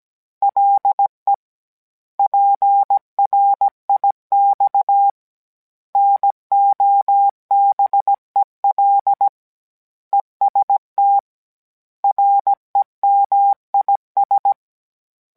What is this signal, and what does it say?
Morse code, heard as a keyed tone, about 17 wpm, 800 Hz. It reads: LE PRIX NOBEL EST REMIS